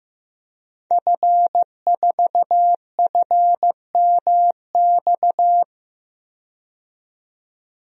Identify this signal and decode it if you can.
Morse code, heard as a keyed tone, about 15 wpm, 700 Hz. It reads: F4FMX
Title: DXCC-100 Call Signs - 5 Characters - 15wpm